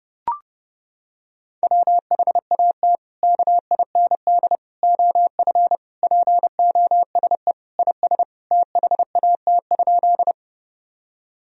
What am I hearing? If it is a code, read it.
Morse code, heard as a keyed tone, about 28 wpm, 700 Hz. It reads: WHAT KIND OF POSE IS THAT?
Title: Sentences from Top 1000 Words - Review Part 2 - 30wpm